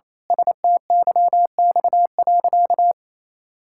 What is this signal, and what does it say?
Morse code, heard as a keyed tone, about 27 wpm, 700 Hz. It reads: STYX.
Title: Pangram Challenge v2 - 28wpm